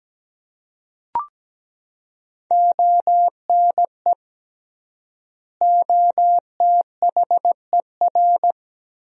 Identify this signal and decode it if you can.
Morse code, heard as a keyed tone, about 17 wpm, 700 Hz. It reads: ONE OTHER
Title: 3x Word-Spacing - Sentences from Top 400 Words - 17wpm